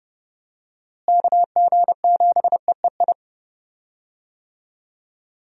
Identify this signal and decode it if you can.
Morse code, heard as a keyed tone, about 30 wpm, 700 Hz. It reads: KG7EEI
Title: USA Call Signs - 6 Characters - Repeated 2x - 30wpm